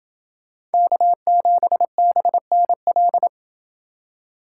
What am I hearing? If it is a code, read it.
Morse code, heard as a keyed tone, about 29 wpm, 700 Hz. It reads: K7BNL